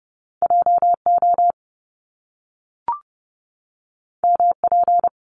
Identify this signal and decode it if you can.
Morse code, heard as a keyed tone, about 30 wpm, 700 Hz. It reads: JO MP
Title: Top Two Letter Combinations in Words 30wpm